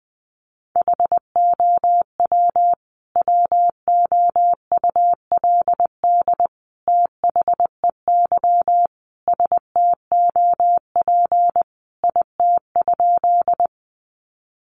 Morse code, heard as a keyed tone, about 20 wpm, 700 Hz. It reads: HOW WOULD THEY STOP IT?
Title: Sentences from Top 300 Words 20wpm